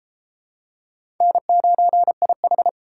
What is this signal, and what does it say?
Morse code, heard as a keyed tone, about 29 wpm, 700 Hz. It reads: N9IH